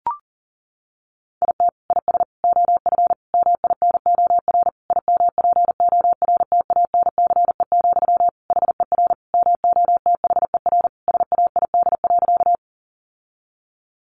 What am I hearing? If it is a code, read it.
Morse code, heard as a keyed tone, about 40 wpm, 700 Hz. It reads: IT IS OF MINOR IMPORTANCE, HER MOTHER SAID.
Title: Sentences from Top 1500 Words 40wpm